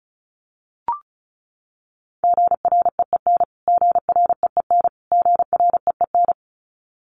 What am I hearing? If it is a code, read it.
Morse code, heard as a keyed tone, about 34 wpm, 700 Hz. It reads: GREEN GREEN GREEN
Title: Top 500 Words Repeated 35wpm